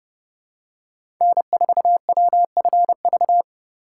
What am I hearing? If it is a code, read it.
Morse code, heard as a keyed tone, about 30 wpm, 700 Hz. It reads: N4WFV
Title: USA Call Signs - 2x Character Spacing - 30wpm